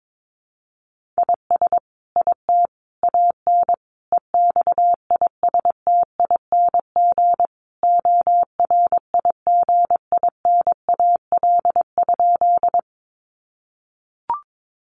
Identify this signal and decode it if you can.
Morse code, heard as a keyed tone, about 22 wpm, 700 Hz. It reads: IS IT AN EXISTING ORIGINAL?
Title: Sentences from Top 1500 Words 22wpm